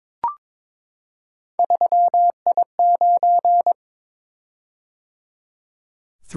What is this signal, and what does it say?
Morse code, heard as a keyed tone, about 21 wpm, 700 Hz. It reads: 3I9
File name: practice / 22wpm / Three Letter-Number Combinations - 22wpm